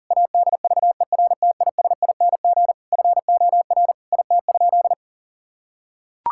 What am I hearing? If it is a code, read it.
Morse code, heard as a keyed tone, about 40 wpm, 700 Hz. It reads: ADVERTISING FOR IT?
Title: Sentences from Top 1000 Words - Review Part 6 - 40wpm